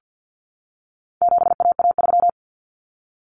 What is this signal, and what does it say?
Morse code, heard as a keyed tone, about 49 wpm, 700 Hz. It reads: 7AA3
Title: Common Cut Numbers - 4-Digit 50wpm